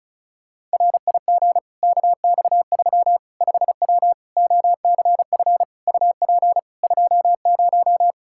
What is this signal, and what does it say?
Morse code, heard as a keyed tone, about 36 wpm, 700 Hz. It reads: RIG KX3 5W OCF UP 20
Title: Practice QSO - Exchange 2 - Info - 35wpm